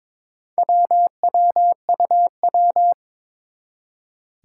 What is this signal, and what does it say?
Morse code, heard as a keyed tone, about 22 wpm, 700 Hz. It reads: WWUW
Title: Four Challenging Letter-Number Combinations 22wpm